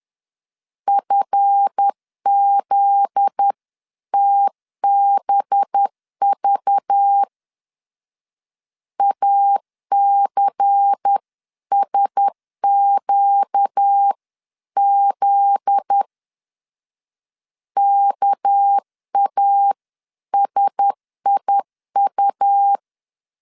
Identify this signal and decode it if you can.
Morse code, heard as a keyed tone, about 10 wpm, 800 Hz. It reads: FZTBV ACSQZ KASIU
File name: LEZIONI / 7 GF9Z / 7 GF9Z 10wpm 2611